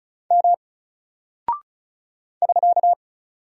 Morse code, heard as a keyed tone, about 36 wpm, 700 Hz. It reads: M <SK>
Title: Single Character (Letter:Number:Punctuation:Prosign) Alt 35wpm